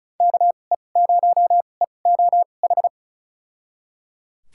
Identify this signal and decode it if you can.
Morse code, heard as a keyed tone, about 34 wpm, spaced as for 25 wpm, 700 Hz. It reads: KE0EOH
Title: USA Call Signs - 6 Characters - 2x Character Spacing - 35wpm